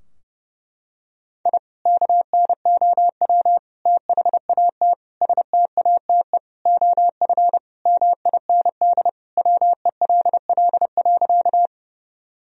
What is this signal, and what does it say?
Morse code, heard as a keyed tone, about 30 wpm, 700 Hz. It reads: I KNOW THAT STATE OF MIND WELL.
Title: Sentences from Top 1000 Words - Review Part 2 - 30wpm